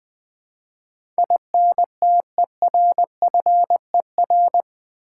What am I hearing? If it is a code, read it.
Morse code, heard as a keyed tone, about 20 wpm, 700 Hz. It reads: INTERFER
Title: Build A Word - 4 Letters At A Time - 12-Letter Words - 20wpm